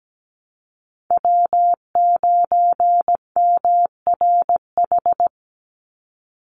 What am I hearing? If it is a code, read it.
Morse code, heard as a keyed tone, about 17 wpm, 700 Hz. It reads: W9MRH
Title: NA SOTA Chaser Callsigns 17wpm